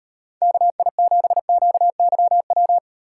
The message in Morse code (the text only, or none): KI7QYW